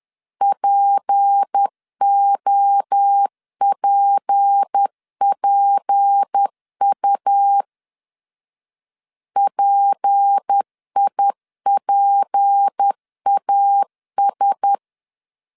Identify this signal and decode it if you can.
Morse code, heard as a keyed tone, about 10 wpm, 800 Hz. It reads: POPPU PIPAS
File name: LEZIONI / 6 PKX0 / 6 P 10wpm 2432